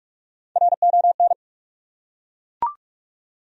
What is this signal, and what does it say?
Morse code, heard as a keyed tone, about 42 wpm, 700 Hz. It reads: RON
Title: USA Common Names - 45wpm